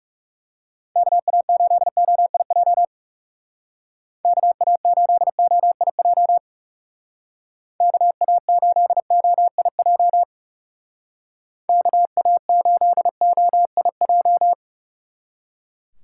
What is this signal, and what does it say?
Morse code, heard as a keyed tone, about 37 wpm, 700 Hz. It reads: KA8OIJ KA8OIJ KA8OIJ KA8OIJ
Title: Speed Racing - USA 2x3 Call Signs 30wpm